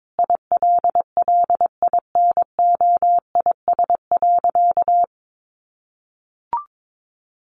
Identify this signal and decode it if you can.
Morse code, heard as a keyed tone, about 22 wpm, 700 Hz. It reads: ILLINOIS.
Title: A Sound Of Thunder 22wpm